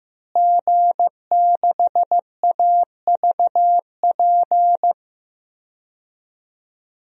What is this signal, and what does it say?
Morse code, heard as a keyed tone, about 15 wpm, 700 Hz. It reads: G6AVP
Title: DXCC-100 Call Signs - 5 Characters - Repeated 3x - 15wpm